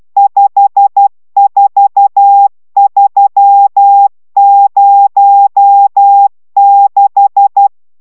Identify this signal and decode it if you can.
Morse code, heard as a keyed tone, about 12 wpm, 800 Hz. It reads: 54306